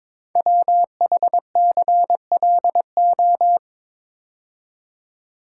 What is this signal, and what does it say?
Morse code, heard as a keyed tone, about 23 wpm, 690 Hz. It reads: WHCLO
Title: Code Groups - 5 Letter:Numbers - 22wpm